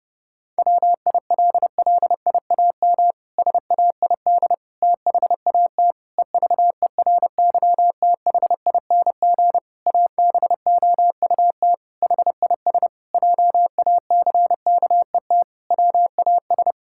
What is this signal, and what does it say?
Morse code, heard as a keyed tone, about 30 wpm, 700 Hz. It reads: WILLIAM SAID THAT EVERYTHING ABOUT HIS JACKET WAS
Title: Pangram Challenge v2 - 30wpm